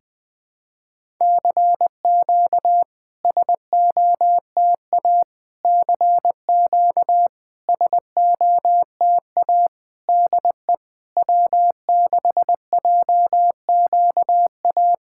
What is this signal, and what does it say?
Morse code, heard as a keyed tone, about 21 wpm, 700 Hz. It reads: CQ SOTA CQ SOTA DE W6JQA
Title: SOTA QSO Practice 20wpm